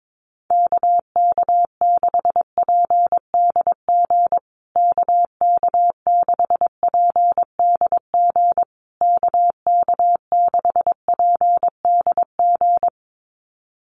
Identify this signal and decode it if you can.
Morse code, heard as a keyed tone, about 22 wpm, 700 Hz. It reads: KK6PDG KK6PDG KK6PDG
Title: USA Call Signs Repeated 3x 22wpm